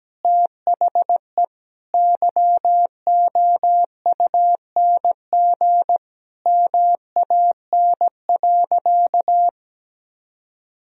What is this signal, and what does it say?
Morse code, heard as a keyed tone, about 17 wpm, 700 Hz. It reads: THE YOUNG MAN.
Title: Sentences from Top 1000 Words - Review Part 2 - 17wpm